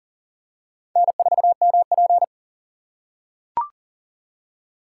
Morse code, heard as a keyed tone, about 40 wpm, 700 Hz. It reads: N4MP
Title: USA Call Signs - 4 Characters - 6x Character Spacing - 40wpm